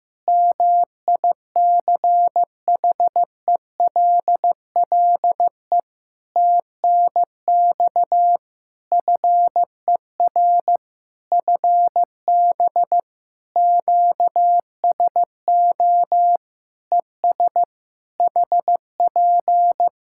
Morse code, heard as a keyed tone, about 15 wpm, 700 Hz. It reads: MICHELLE TNX FER FB QSO ES HP
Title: Practice QSO - Exchange 4 - Ending - 15wpm